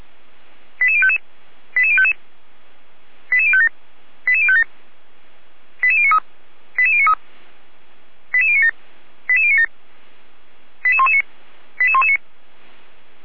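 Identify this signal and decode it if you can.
Five-tone selective call (ZVEI), heard as quick runs of five tones, tones of about 80 ms. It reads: 88055 88055 88057 88057 88093 88093 88097 88097 88119 88119
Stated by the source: ZVEI five-tone call received by SDR, from the audio server of Feuerwehr Orlamünde.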